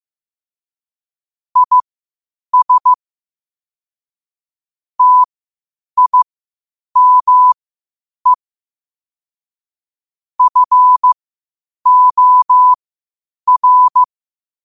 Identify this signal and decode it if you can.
Morse code, heard as a keyed tone, about 15 wpm, spaced as for 8 wpm, 1000 Hz. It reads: IS TIME FOR
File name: qotdfiles / 8wpm